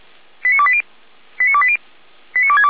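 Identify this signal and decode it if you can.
Five-tone selective call (ZVEI), heard as quick runs of five tones, tones of about 70 ms. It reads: 89280 89280 89262
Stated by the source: ZVEI five-tone call received on an SDR.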